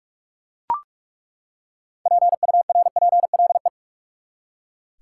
Morse code, heard as a keyed tone, about 41 wpm, 700 Hz. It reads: PURPLE
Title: Six Letter Words - 45wpm